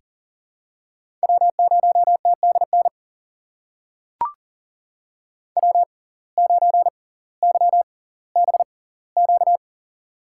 Morse code, heard as a keyed tone, about 40 wpm, spaced as for 20 wpm, 700 Hz. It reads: W0TDN W9YBQ
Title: USA Call Signs - 6x Character Spacing - 40wpm